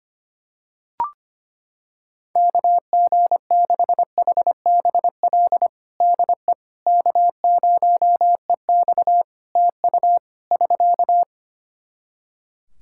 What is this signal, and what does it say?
Morse code, heard as a keyed tone, about 25 wpm, 700 Hz. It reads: KG6HBL DE K0EX TU <SK>
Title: K1USN SST QSO Practice 25wpm